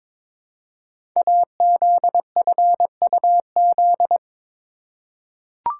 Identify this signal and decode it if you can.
Morse code, heard as a keyed tone, about 22 wpm, 700 Hz. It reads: AZFUZ
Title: Code Groups - 5 Letters - 22wpm